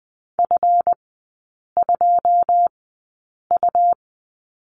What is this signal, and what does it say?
Morse code, heard as a keyed tone, about 20 wpm, spaced as for 8 wpm, 700 Hz. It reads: F2U
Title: 138 Lesson 23e - Letter f - Three Characters 20wpm